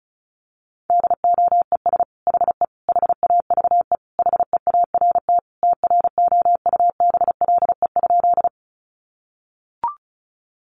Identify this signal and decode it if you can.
Morse code, heard as a keyed tone, about 34 wpm, 700 Hz. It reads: DOES HE HAVE HEART TROUBLE?
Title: Sentences from Top 1200 Words 35wpm